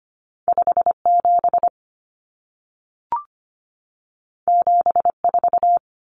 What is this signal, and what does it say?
Morse code, heard as a keyed tone, about 25 wpm, 700 Hz. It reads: 57 74
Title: Common Cut Numbers - 2-Digit 25wpm